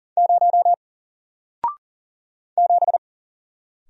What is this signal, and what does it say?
Morse code, heard as a keyed tone, about 40 wpm, 700 Hz. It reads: 0 7
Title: Single Letter-Number Alt 40wpm